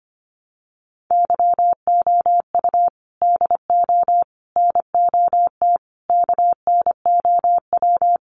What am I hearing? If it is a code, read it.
Morse code, heard as a keyed tone, about 25 wpm, 700 Hz. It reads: YOU DO NOT KNOW
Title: Sentences from Top 600 Words 25wpm